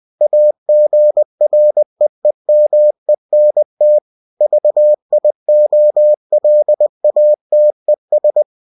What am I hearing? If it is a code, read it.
Morse code, heard as a keyed tone, about 20 wpm, 600 Hz. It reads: AGREEMENT VIOLATES
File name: cw-books / winnie-the-pooh / winnie-20wpm-page-0119